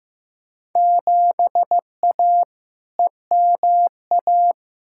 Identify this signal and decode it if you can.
Morse code, heard as a keyed tone, about 16 wpm, 700 Hz. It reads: 7A EMA
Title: Field Day Exchanges Encore - 15wpm